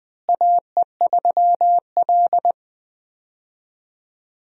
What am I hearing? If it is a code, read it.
Morse code, heard as a keyed tone, about 20 wpm, 700 Hz. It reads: AE3L
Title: USA Call Signs - 4 Characters - Repeated 3x - 20wpm